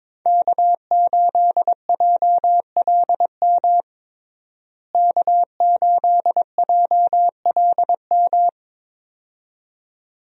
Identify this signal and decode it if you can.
Morse code, heard as a keyed tone, about 21 wpm, 700 Hz. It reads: K8JLM K8JLM
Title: USA Call Signs - 5 Characters - Repeated 2x - 22wpm